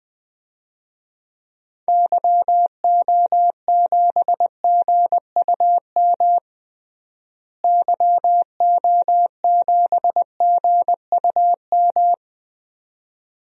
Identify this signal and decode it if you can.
Morse code, heard as a keyed tone, about 21 wpm, 700 Hz. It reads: YO7GUM YO7GUM
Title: DXCC-100 Call Signs - 6 Characters - Repeated 2x - 20wpm